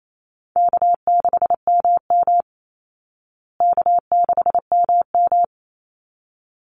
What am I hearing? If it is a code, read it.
Morse code, heard as a keyed tone, about 28 wpm, 700 Hz. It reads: K6MM K6MM
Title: Call Signs Repeated 3x - 28wpm